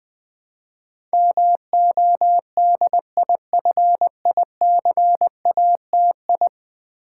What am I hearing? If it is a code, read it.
Morse code, heard as a keyed tone, about 21 wpm, 700 Hz. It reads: MODIFICATI
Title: Build A Word - 5 Letters At A Time - 12-Letter Words - 20wpm